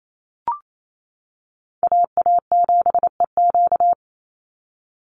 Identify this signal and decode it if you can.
Morse code, heard as a keyed tone, about 28 wpm, 700 Hz. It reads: AA7EQ
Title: NA SOTA Chaser Callsigns 28wpm